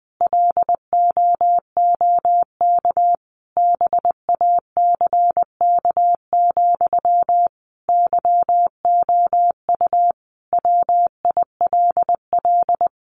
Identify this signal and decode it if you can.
Morse code, heard as a keyed tone, about 19 wpm, 700 Hz. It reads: LOOK BACK, YOU WILL
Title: Sentences from Top 800 Words 20wpm